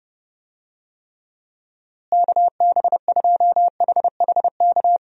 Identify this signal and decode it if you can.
Morse code, heard as a keyed tone, about 30 wpm, 700 Hz. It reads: KB2HHK
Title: USA Call Signs - 6 Characters - Repeated 2x - 30wpm